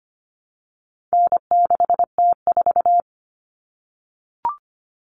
Morse code, heard as a keyed tone, about 25 wpm, 700 Hz. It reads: N6T4
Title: All Cut Numbers - 4-Digit 25wpm